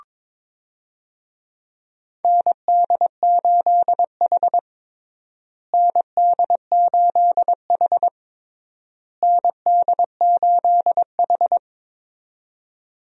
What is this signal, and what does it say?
Morse code, heard as a keyed tone, about 22 wpm, 700 Hz. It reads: ND8H ND8H ND8H
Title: USA Call Signs - 4 Characters - Repeated 3x - 22wpm